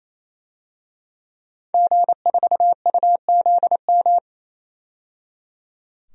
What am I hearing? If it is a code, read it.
Morse code, heard as a keyed tone, about 28 wpm, 700 Hz. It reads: G4UZM
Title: DXCC-100 Call Signs - 5 Characters - 28wpm